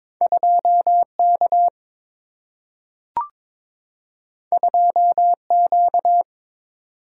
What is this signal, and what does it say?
Morse code, heard as a keyed tone, about 22 wpm, 700 Hz. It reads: 2 K 2 Q
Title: Single-Letter Single-Number 22wpm